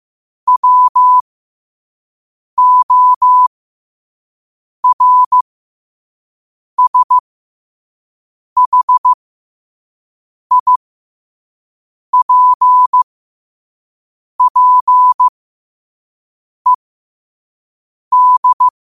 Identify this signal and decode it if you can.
Morse code, heard as a keyed tone, about 15 wpm, spaced as for 5 wpm, 1000 Hz. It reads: WORSHIPPED
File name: qotdfiles / 5wpm